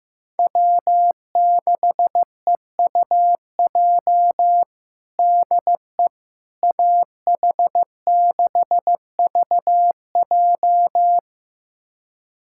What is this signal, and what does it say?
Morse code, heard as a keyed tone, about 16 wpm, 700 Hz. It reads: W6EUJ DE AH6VJ
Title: USA Callsign DE Callsign - 5 Characters - 15wpm